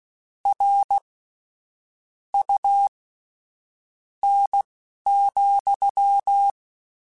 Morse code, heard as a keyed tone, about 16 wpm, spaced as for 7 wpm, 800 Hz. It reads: RUN,
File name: Misc / Misc-mp3s / WordPairs sp @10